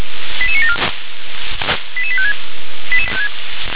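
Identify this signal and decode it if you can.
Five-tone selective call (ZVEI), heard as quick runs of five tones, tones of about 70 ms. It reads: 88974 88856
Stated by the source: ZVEI five-tone selective call received on an SDR.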